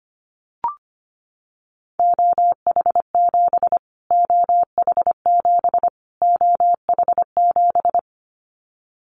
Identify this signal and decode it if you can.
Morse code, heard as a keyed tone, about 25 wpm, 700 Hz. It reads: OH7 OH7 OH7
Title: Callsign Prefixes for Top 25 DXCC - Repeated 3x - 25wpm